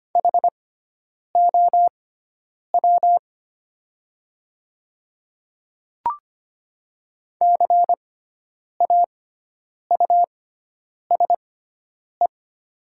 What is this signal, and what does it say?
Morse code, heard as a keyed tone, about 25 wpm, spaced as for 9 wpm, 690 Hz. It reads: HOW CAUSE
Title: 6x Character Spacing - Top 200 Words - 25wpm